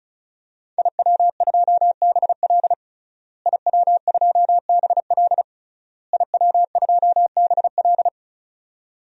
Morse code, heard as a keyed tone, about 36 wpm, 700 Hz. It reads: IW2BL IW2BL IW2BL
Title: DXCC-100 Call Signs - 5 Characters - Repeated 3x - 35wpm